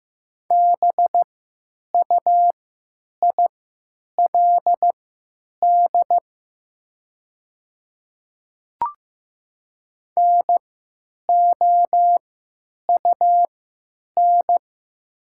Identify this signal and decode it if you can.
Morse code, heard as a keyed tone, about 15 wpm, spaced as for 9 wpm, 700 Hz. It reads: BUILD NOUN
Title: 3x Character Spacing - Top 500 Words - 15wpm